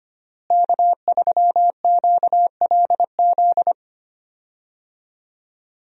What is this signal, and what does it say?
Morse code, heard as a keyed tone, about 25 wpm, 700 Hz. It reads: K3QLZ
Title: USA Call Signs - 5 Characters - Repeated 2x - 25wpm